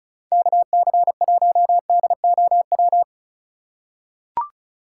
Morse code, heard as a keyed tone, about 36 wpm, 700 Hz. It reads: KC1DOW
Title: USA Call Signs - 3x Character Spacing - 35wpm